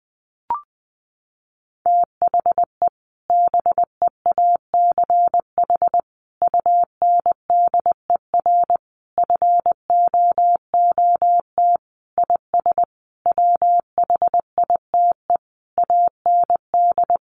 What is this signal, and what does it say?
Morse code, heard as a keyed tone, about 20 wpm, 700 Hz. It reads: THE BEACH UNDER FOOT IS WHITE AND
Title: Sentences from Top 1500 Words 20wpm